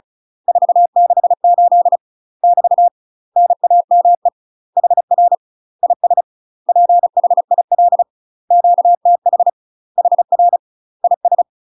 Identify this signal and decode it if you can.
Morse code, heard as a keyed tone, about 36 wpm, 700 Hz. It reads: 468 <BT> NAME HR IS PHIL QTH HR IS
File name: UK MASTER QSOs TRAINING / 35WPM / UK MASTER QSOs 0031